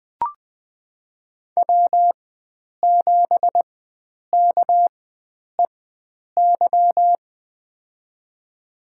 Morse code, heard as a keyed tone, about 20 wpm, spaced as for 9 wpm, 700 Hz. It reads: W7KEY
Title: USA Call Signs - 5 Characters - 4x Character Spacing - 20wpm